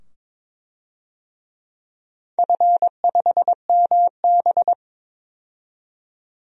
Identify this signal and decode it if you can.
Morse code, heard as a keyed tone, about 22 wpm, 700 Hz. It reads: F5MB
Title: DXCC-100 Call Signs - 4 Characters - 22wpm